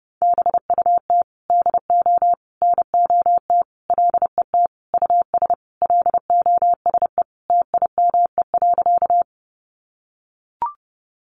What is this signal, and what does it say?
Morse code, heard as a keyed tone, about 30 wpm, 700 Hz. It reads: BUT DO NOT LET US LOSE TIME.
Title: Sentences from Top 1100 Words 30wpm